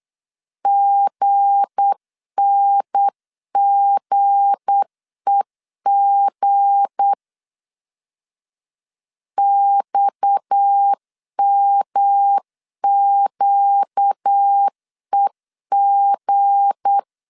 Morse code, heard as a keyed tone, about 8 wpm, 800 Hz. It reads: GNGEG XMQEG
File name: LEZIONI / 7 GF9Z / 7 G 8wpm 3118